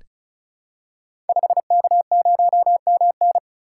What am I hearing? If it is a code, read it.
Morse code, heard as a keyed tone, about 34 wpm, 700 Hz. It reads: 5K0MN
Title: Code Groups - 5 Letter:Numbers - 35wpm